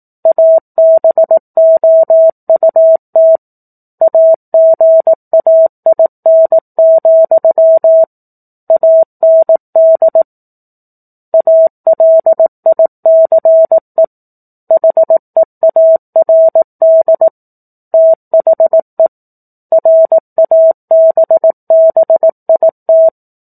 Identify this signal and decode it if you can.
Morse code, heard as a keyed tone, about 18 wpm, 650 Hz. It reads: ABOUT AGAIN, AND ALICE HEARD THE RABBIT
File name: 04-Chapter-04 18 WPM